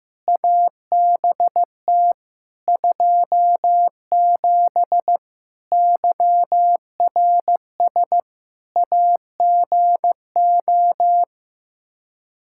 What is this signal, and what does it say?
Morse code, heard as a keyed tone, about 16 wpm, 700 Hz. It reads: ABT 27 YRS AGO
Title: K1USN SST QSO Practice 15wpm